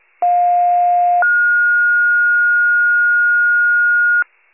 Two-tone page, heard as a one-second tone, then a three-second tone, about 710 then 1430 Hz. It is heard from 0.2 to 4.2 s.
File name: wcalert